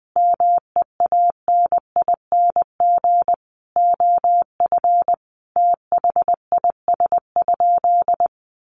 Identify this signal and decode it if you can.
Morse code, heard as a keyed tone, about 20 wpm, 700 Hz. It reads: MEANING OF THIS?
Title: Sentences from Top 900 Words 20wpm